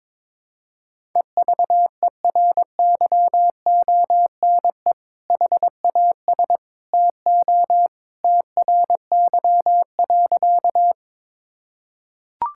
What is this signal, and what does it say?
Morse code, heard as a keyed tone, about 22 wpm, 700 Hz. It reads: EVERYONE HAS TO TRY.
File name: practice / 22wpm / Sentences with 2-Word Phrases - 22wpm